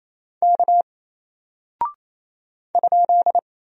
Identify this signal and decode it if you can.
Morse code, heard as a keyed tone, about 28 wpm, 700 Hz. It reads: K ?
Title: Single Character (Letter:Number:Punctuation:Prosign) Alt 28wpm